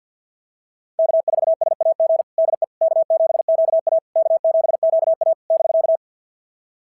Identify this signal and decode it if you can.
Morse code, heard as a keyed tone, about 48 wpm, 690 Hz. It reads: K4SAG DE K7QA K7QA <BK>
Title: SOTA QSO Practice 50wpm